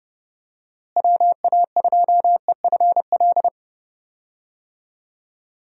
Morse code, heard as a keyed tone, about 32 wpm, 700 Hz. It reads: WA2EFL
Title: USA Call Signs - 6 Characters - Repeated 3x - 30wpm